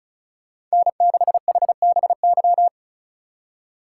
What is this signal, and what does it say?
Morse code, heard as a keyed tone, about 36 wpm, 700 Hz. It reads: N6HBY
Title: USA Call Signs - 2x Character Spacing - 35wpm